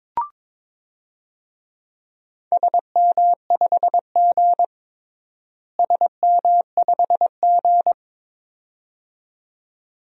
Morse code, heard as a keyed tone, about 22 wpm, 700 Hz. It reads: SM5G SM5G
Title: DXCC-100 Call Signs - 4 Characters - Repeated 2x - 22wpm